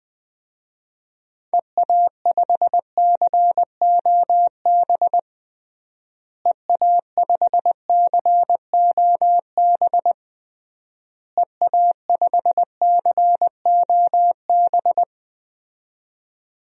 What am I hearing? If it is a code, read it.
Morse code, heard as a keyed tone, about 21 wpm, 700 Hz. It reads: EA5COB EA5COB EA5COB
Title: DXCC-100 Call Signs - 6 Characters - Repeated 3x - 20wpm